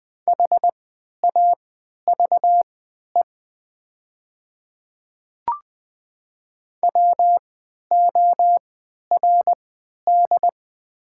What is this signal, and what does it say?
Morse code, heard as a keyed tone, about 20 wpm, spaced as for 11 wpm, 700 Hz. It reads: HAVE WORD
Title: 3x Character Spacing - Top 100 Words - 20wpm